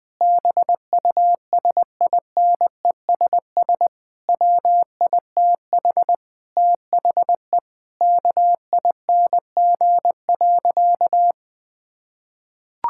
Morse code, heard as a keyed tone, about 19 wpm, 700 Hz. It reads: BUSINESS WITH THE KING.
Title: Sentences from Top 1000 Words - Review Part 6 - 20wpm